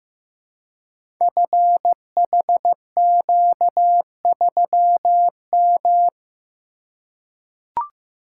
Morse code, heard as a keyed tone, about 15 wpm, 700 Hz. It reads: FHQ3M
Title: Code Groups - 5 Letter:Numbers - 15wpm